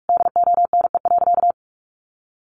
Morse code, heard as a keyed tone, about 44 wpm, 700 Hz. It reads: DONE.
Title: Sentences from Top 500 Words 45wpm